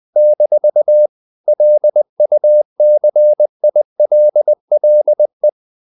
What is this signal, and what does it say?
Morse code, heard as a keyed tone, about 20 wpm, 600 Hz. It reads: - LUCILLE
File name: CW / Practice / EnglishQuotations0320